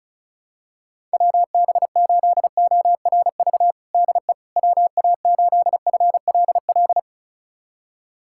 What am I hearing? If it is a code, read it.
Morse code, heard as a keyed tone, about 34 wpm, 700 Hz. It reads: WB8ORV DE WA8FLL
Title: USA Callsign DE Callsign - 6 Characters - 35wpm